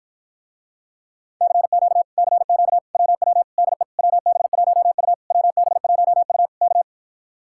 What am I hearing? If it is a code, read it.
Morse code, heard as a keyed tone, about 53 wpm, 700 Hz. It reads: CQ CQ WW DE WB1U WB1U K